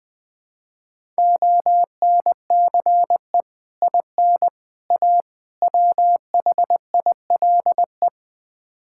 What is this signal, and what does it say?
Morse code, heard as a keyed tone, about 20 wpm, 700 Hz. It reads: ONCE IN A WHILE
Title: Build A Phrase - 4-Word Phrases - 20wpm